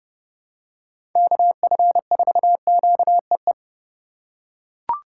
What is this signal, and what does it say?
Morse code, heard as a keyed tone, about 30 wpm, 700 Hz. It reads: KF4QEE
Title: USA Call Signs - 6 Characters - 2x Character Spacing - 30wpm